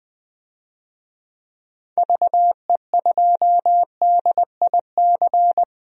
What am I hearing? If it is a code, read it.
Morse code, heard as a keyed tone, about 20 wpm, 700 Hz. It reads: VE2DIC
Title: DXCC-100 Call Signs - 6 Characters - 20wpm